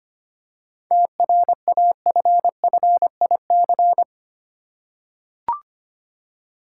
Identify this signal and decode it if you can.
Morse code, heard as a keyed tone, about 25 wpm, 700 Hz. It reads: TRAFFIC
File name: practice / 25wpm / Easy Words with Hints - 25wpm